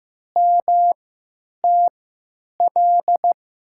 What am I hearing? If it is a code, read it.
Morse code, heard as a keyed tone, about 15 wpm, spaced as for 9 wpm, 700 Hz. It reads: MTL
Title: USA Call Signs - 6 Characters - 3x Character Spacing - 15wpm